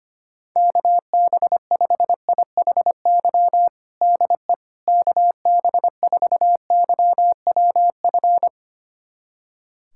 Morse code, heard as a keyed tone, about 25 wpm, 700 Hz. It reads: KB5IHY DE KB4YWF
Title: USA Callsign DE Callsign - 6 Characters - 25wpm